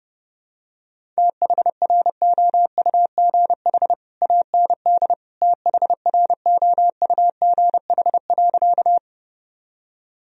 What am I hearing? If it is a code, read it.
Morse code, heard as a keyed tone, about 30 wpm, 700 Hz. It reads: THROUGH AND THROUGH.
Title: Sentences from Top 1000 Words - Review Part 1 - 30wpm